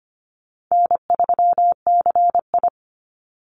Morse code, heard as a keyed tone, about 25 wpm, 700 Hz. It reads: N3CI
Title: Call Signs - 25wpm